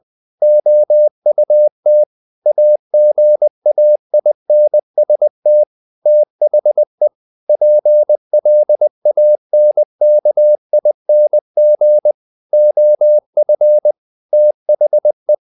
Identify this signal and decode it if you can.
Morse code, heard as a keyed tone, about 20 wpm, 600 Hz. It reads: OUT AGAINST THE PLANKING OF THE
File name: the old man and the sea / the old man and the sea part 1 20wpm